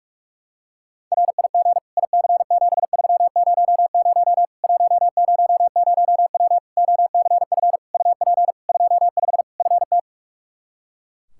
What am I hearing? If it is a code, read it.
Morse code, heard as a keyed tone, about 41 wpm, 700 Hz. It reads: RIG IC7300 100W OCF UP 25 FT
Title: Practice QSO - Exchange 2 - Info - 45wpm